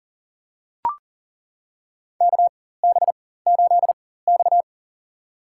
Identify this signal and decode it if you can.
Morse code, heard as a keyed tone, about 40 wpm, spaced as for 19 wpm, 700 Hz. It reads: KB8X
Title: USA Call Signs - 4 Characters - 4x Character Spacing - 40wpm